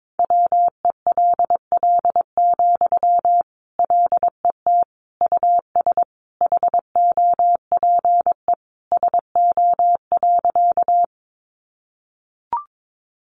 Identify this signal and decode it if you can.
Morse code, heard as a keyed tone, about 22 wpm, 700 Hz. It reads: WELL, LET US HOPE SO.
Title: Sentences from Top 1000 Words 22wpm